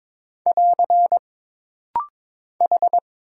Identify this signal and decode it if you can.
Morse code, heard as a keyed tone, about 22 wpm, 700 Hz. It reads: <AR> H
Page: This rapid-fire timing is geared to push your ICR (Instant Character Recognition) skills.